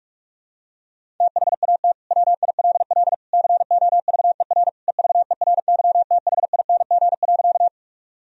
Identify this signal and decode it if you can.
Morse code, heard as a keyed tone, about 46 wpm, 700 Hz. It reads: THAT WILL COVER EVERYTHING.
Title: Sentences from Top 1000 Words - Review Part 1 - 45wpm